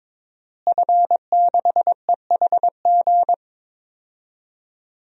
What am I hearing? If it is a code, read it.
Morse code, heard as a keyed tone, about 22 wpm, 700 Hz. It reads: F6EHG
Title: DXCC-100 Call Signs - 5 Characters - Repeated 3x - 22wpm